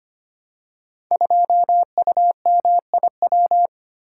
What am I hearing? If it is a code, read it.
Morse code, heard as a keyed tone, about 25 wpm, 700 Hz. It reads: 2UMIW
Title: Code Groups - 5 Letter:Numbers - 25wpm